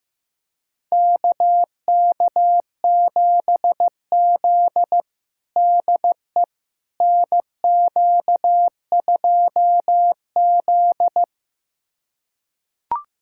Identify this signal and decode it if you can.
Morse code, heard as a keyed tone, about 15 wpm, 700 Hz. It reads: KK7Z DE NQ2Z
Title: USA Callsign DE Callsign - 4 Characters - 15wpm